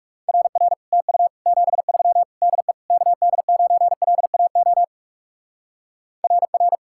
Morse code, heard as a keyed tone, about 38 wpm, 700 Hz. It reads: RR TU 73 DE KD9LAO RR